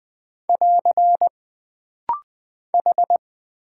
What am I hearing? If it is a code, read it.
Morse code, heard as a keyed tone, about 20 wpm, 700 Hz. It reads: <AR> H
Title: Single Character (Letter:Number:Punctuation:Prosign) - Rapid-Fire - 20wpm